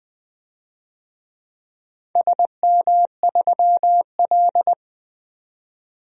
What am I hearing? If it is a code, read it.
Morse code, heard as a keyed tone, about 20 wpm, 700 Hz. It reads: SM3L
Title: DXCC-100 Call Signs - 4 Characters - 20wpm